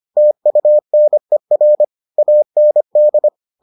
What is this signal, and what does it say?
Morse code, heard as a keyed tone, about 25 wpm, 600 Hz. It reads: TUNER AND